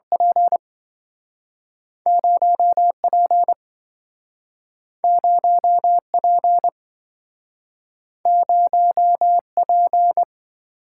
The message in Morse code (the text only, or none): P0 P0 P0 P